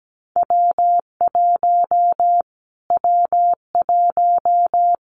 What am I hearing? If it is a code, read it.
Morse code, heard as a keyed tone, about 17 wpm, 700 Hz. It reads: W1 W1
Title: Callsign Prefixes for Top 25 DXCC - Repeated 3x - 17wpm